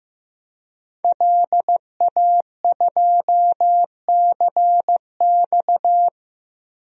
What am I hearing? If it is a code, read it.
Morse code, heard as a keyed tone, about 15 wpm, 700 Hz. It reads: LA2CX
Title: DXCC-100 Call Signs - 5 Characters - Repeated 2x - 15wpm